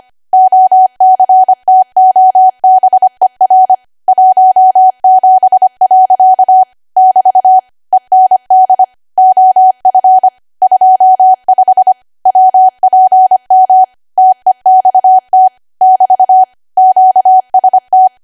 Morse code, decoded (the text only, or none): OCTOBER 17. <BT> END OF 25 WPM TEXT <BT> QST